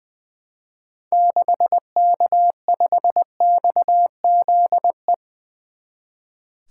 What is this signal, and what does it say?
Morse code, heard as a keyed tone, about 20 wpm, 700 Hz. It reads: 6K5XZE
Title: 2023 SOTA Activator Call Signs - 20wpm